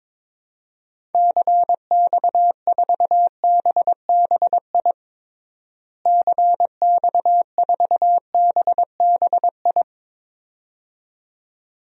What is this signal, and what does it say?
Morse code, heard as a keyed tone, about 22 wpm, 700 Hz. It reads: CX4BBI CX4BBI
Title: DXCC-100 Call Signs - 6 Characters - Repeated 2x - 22wpm